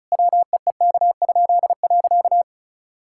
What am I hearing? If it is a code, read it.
Morse code, heard as a keyed tone, about 36 wpm, 700 Hz. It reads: WEEK?.
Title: Pangram Challenge v2 - 35wpm